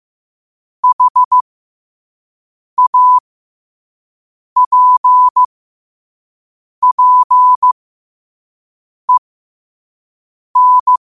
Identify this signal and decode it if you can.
Morse code, heard as a keyed tone, about 15 wpm, spaced as for 5 wpm, 1000 Hz. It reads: HAPPEN
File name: qotdfiles / 5wpm